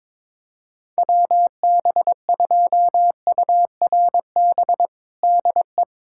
Morse code, heard as a keyed tone, about 22 wpm, 700 Hz. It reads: WB2URB DE
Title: USA Callsign DE Callsign 22wpm